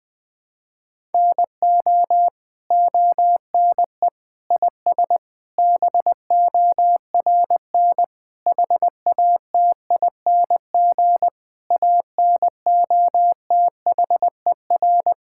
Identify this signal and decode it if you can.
Morse code, heard as a keyed tone, about 21 wpm, 700 Hz. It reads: NO ONE IS BORN HATING ANOTHER
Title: Sentences with 9-Word Phrases - 20wpm